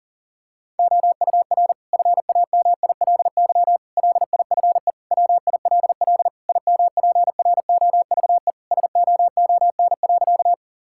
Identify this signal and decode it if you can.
Morse code, heard as a keyed tone, about 40 wpm, 700 Hz. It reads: OUR FAMILY LIFE WILL IMPROVE SOON.
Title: Sentences from Top 1000 Words - Review Part 5 - 40wpm